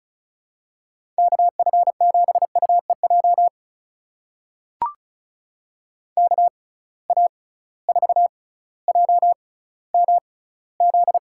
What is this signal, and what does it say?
Morse code, heard as a keyed tone, about 36 wpm, spaced as for 18 wpm, 700 Hz. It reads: KF7UEJ KA4JMZ
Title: USA Call Signs - 6 Characters - 6x Character Spacing - 35wpm